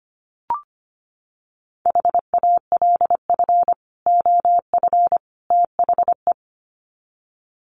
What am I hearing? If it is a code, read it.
Morse code, heard as a keyed tone, about 25 wpm, 700 Hz. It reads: HALF OF THE
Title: Sets of 3 Words 25wpm